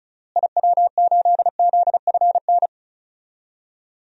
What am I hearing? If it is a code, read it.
Morse code, heard as a keyed tone, about 36 wpm, 700 Hz. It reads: IW8ZFN
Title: DXCC-100 Call Signs - 6 Characters - Repeated 3x - 35wpm